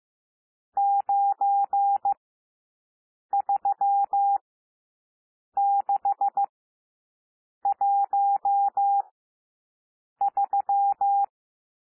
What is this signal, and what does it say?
Morse code, heard as a keyed tone, about 15 wpm, spaced as for 6 wpm, 800 Hz. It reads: 93613
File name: cwmp3 / N1 SOUND NUMBERS 15 wpm